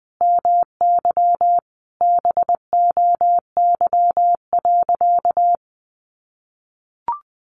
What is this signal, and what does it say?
Morse code, heard as a keyed tone, about 20 wpm, 700 Hz. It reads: MY BOY.
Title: Sentences from Top 700 Words 20wpm